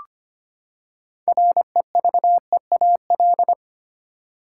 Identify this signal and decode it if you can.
Morse code, heard as a keyed tone, about 25 wpm, 700 Hz. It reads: REVEAL
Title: Six Letter Words - 25wpm